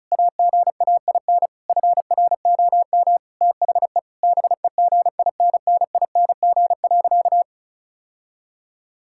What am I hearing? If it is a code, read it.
Morse code, heard as a keyed tone, about 36 wpm, 700 Hz. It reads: AGAIN FROM THE BEGINNING.
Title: Sentences from Top 1000 Words - Review Part 5 - 35wpm